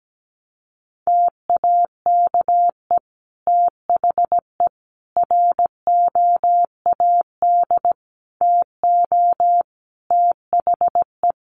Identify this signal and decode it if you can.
Morse code, heard as a keyed tone, about 17 wpm, 700 Hz. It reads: TAKE THE ROAD TO THE
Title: Sentences from Top 400 Words 17wpm